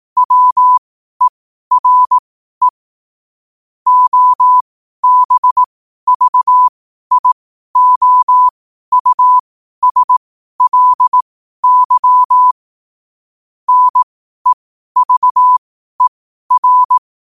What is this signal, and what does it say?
Morse code, heard as a keyed tone, about 18 wpm, spaced as for 12 wpm, 1000 Hz. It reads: WERE OBVIOUSLY NEVER